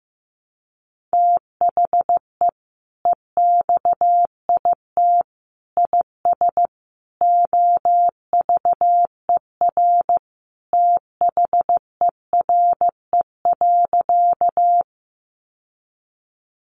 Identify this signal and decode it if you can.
Morse code, heard as a keyed tone, about 15 wpm, 700 Hz. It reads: THE EXIT IS OVER THERE.
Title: Sentences from Top 1100 Words 15wpm